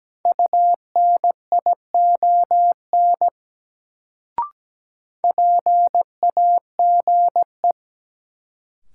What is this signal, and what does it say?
Morse code, heard as a keyed tone, about 17 wpm, 700 Hz. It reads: UNION PAGE
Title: Sets of 5 Words Encore - New Word Drill - 17wpm